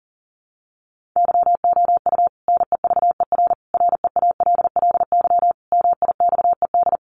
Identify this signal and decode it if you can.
Morse code, heard as a keyed tone, about 40 wpm, 700 Hz. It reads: YOU NEVER REALLY MIXED
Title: Sentences from Top 1500 Words 40wpm